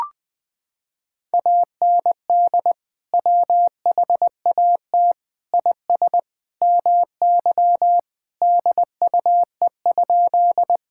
Morse code, heard as a keyed tone, about 20 wpm, 700 Hz. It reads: AND WHAT IS MY DUE?
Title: Sentences from Top 1000 Words - Review Part 6 - 20wpm